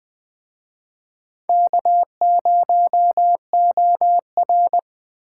Morse code, heard as a keyed tone, about 22 wpm, 700 Hz. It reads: K0OR